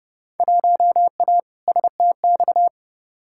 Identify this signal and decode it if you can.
Morse code, heard as a keyed tone, about 32 wpm, 700 Hz. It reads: 1A STX
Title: Field Day Exchanges - 30wpm